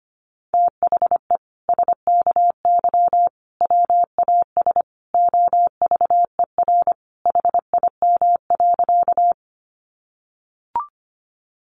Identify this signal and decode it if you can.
Morse code, heard as a keyed tone, about 25 wpm, 700 Hz. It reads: THE SKY WAS OVER HIM.
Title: Sentences from Top 1200 Words 25wpm